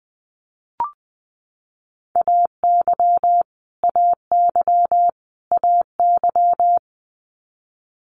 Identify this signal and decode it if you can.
Morse code, heard as a keyed tone, about 20 wpm, 700 Hz. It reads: AY AY AY
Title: Multi-letter Phonemes Repeated 3x 20wpm